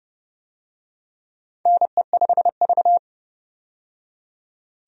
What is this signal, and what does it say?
Morse code, heard as a keyed tone, about 30 wpm, 700 Hz. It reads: NE5V
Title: USA Call Signs - 4 Characters - 30wpm